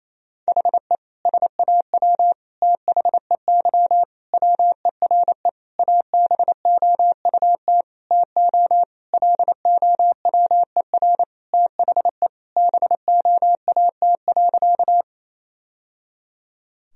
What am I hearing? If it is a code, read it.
Morse code, heard as a keyed tone, about 28 wpm, 700 Hz. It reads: HE SAW THEY WERE ABOUT TO LOWER THE BOAT.
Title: Sentences from Top 1000 Words - Review Part 4 - 28wpm